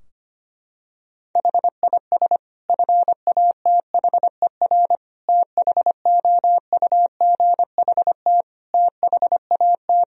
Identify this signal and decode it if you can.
Morse code, heard as a keyed tone, about 25 wpm, 700 Hz. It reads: HIS FATHER THOUGHT THAT
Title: Sentences from Top 1000 Words - Review Part 4 - 25wpm